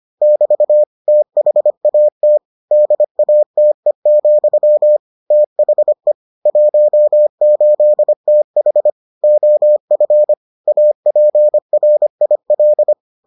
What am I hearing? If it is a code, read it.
Morse code, heard as a keyed tone, about 25 wpm, 600 Hz. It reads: <BT> THAT DATE, THE 18TH OF APRIL